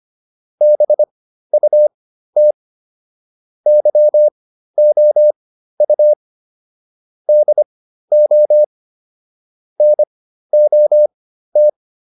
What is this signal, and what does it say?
Morse code, heard as a keyed tone, about 25 wpm, spaced as for 13 wpm, 600 Hz. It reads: BUT YOU DO NOT